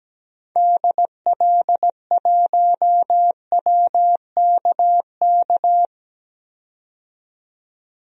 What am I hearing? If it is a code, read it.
Morse code, heard as a keyed tone, about 17 wpm, 700 Hz. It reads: DL1WKK
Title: DXCC-100 Call Signs - 6 Characters - 17wpm